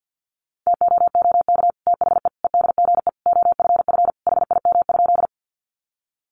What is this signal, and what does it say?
Morse code, heard as a keyed tone, about 50 wpm, 700 Hz. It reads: TOOK THE EDGE OFF HIM?
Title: Sentences from Top 1300 Words 50wpm